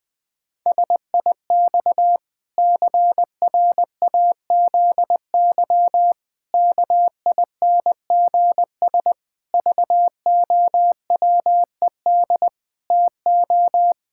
Morse code, heard as a keyed tone, about 20 wpm, 700 Hz. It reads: SIX CRAZY KINGS VOWED TO
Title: Pangram Challenge v2 - 20wpm